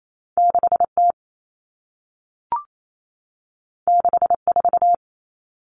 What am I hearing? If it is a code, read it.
Morse code, heard as a keyed tone, about 28 wpm, 700 Hz. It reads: 6T 64
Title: Common Cut Numbers - 2-Digit 28wpm